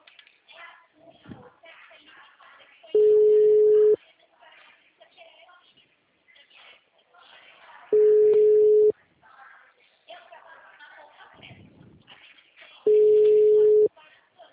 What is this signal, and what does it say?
Ringback tone (Europe, 425 Hz), with speech.